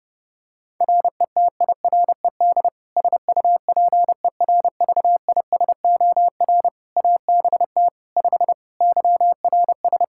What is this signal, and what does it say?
Morse code, heard as a keyed tone, about 30 wpm, 700 Hz. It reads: RETIRED SUPERVISOR ABT 5 YRS
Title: Practice QSO - Exchange 3 - More Info - 30wpm